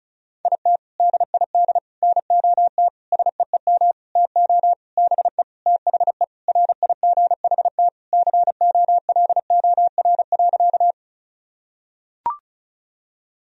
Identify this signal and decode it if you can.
Morse code, heard as a keyed tone, about 36 wpm, 700 Hz. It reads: IT DID NOT SEEM TO BE THE RIGHT COLOR.
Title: Sentences from Top 1000 Words - Review Part 2 - 35wpm